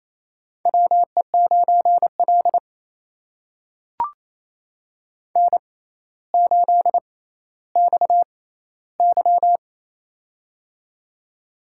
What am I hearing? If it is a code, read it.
Morse code, heard as a keyed tone, about 27 wpm, spaced as for 14 wpm, 700 Hz. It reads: WE9L N8XY
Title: USA Call Signs - 4 Characters - 6x Character Spacing - 28wpm